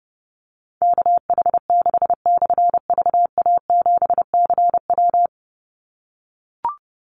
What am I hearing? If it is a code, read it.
Morse code, heard as a keyed tone, about 28 wpm, 700 Hz. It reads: KH6/VA7CW
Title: Call Signs - 30wpm